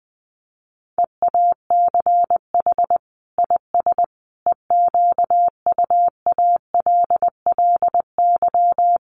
Morse code, heard as a keyed tone, about 20 wpm, 700 Hz. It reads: EACH IS EQUALLY
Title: Sentences from Top 1400 Words 20wpm